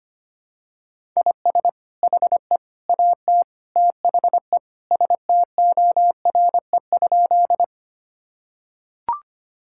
Morse code, heard as a keyed tone, about 25 wpm, 700 Hz. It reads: IS HE AT THE STORE?
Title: Sentences from Top 1000 Words - Review Part 5 - 25wpm